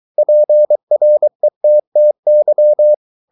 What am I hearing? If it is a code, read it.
Morse code, heard as a keyed tone, about 23 wpm, 600 Hz. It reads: PRETTY